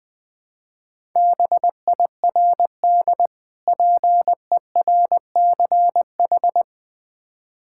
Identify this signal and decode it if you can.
Morse code, heard as a keyed tone, about 20 wpm, 700 Hz. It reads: BIRD PERCH
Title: Difficult Words with Hints - 20wpm